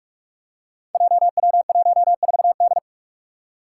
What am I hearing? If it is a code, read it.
Morse code, heard as a keyed tone, about 46 wpm, 700 Hz. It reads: JW14D
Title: Five Challenging Letter-Number Combinations 45wpm